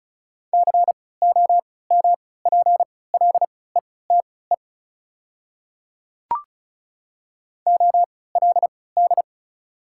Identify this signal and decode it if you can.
Morse code, heard as a keyed tone, about 36 wpm, spaced as for 20 wpm, 700 Hz. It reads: COMPLETE OLD
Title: 3x Character Spacing - Top 500 Words - 35wpm